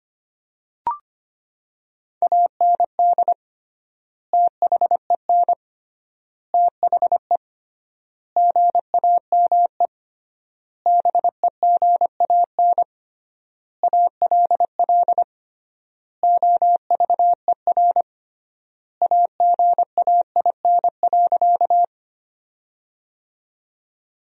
Morse code, heard as a keyed tone, about 25 wpm, 700 Hz. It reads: AND THEN THE GAME BEGAN ALL OVER AGAIN.
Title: 3x Word-Spacing - Sentences from Top 500 Words - 25wpm